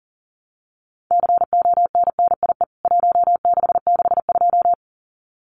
Morse code, heard as a keyed tone, about 40 wpm, 700 Hz. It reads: CONNIE 1662
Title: CWT Exchange 40wpm